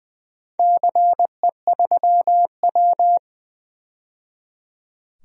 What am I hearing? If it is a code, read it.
Morse code, heard as a keyed tone, about 20 wpm, 700 Hz. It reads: CE3W
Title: DXCC-100 Call Signs - 4 Characters - 20wpm